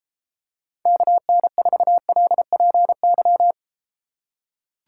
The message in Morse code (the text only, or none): KN4LPY